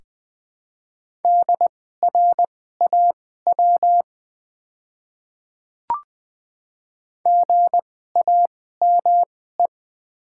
Morse code, heard as a keyed tone, about 20 wpm, spaced as for 14 wpm, 700 Hz. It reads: DRAW GAME
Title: 2x Character Spacing - Top 500 Words - 20wpm